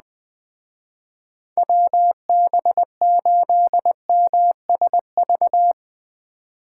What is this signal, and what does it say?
Morse code, heard as a keyed tone, about 20 wpm, 700 Hz. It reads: WB8MSV
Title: Speed Racing - USA 2x3 Call Signs 17wpm